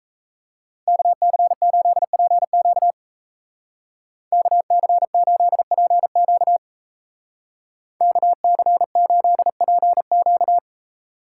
Morse code, heard as a keyed tone, about 34 wpm, 700 Hz. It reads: KC8PQ KC8PQ KC8PQ
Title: Speed Racing - USA Call Signs 28wpm